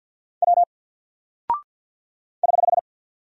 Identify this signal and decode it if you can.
Morse code, heard as a keyed tone, about 48 wpm, 700 Hz. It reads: W <HH>
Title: Single Character (Letter:Number:Punctuation:Prosign) Alt 50wpm